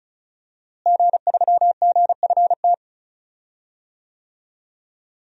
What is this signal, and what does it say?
Morse code, heard as a keyed tone, about 35 wpm, 700 Hz. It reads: G3GFT
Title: DXCC-100 Call Signs - 5 Characters - Repeated 2x - 35wpm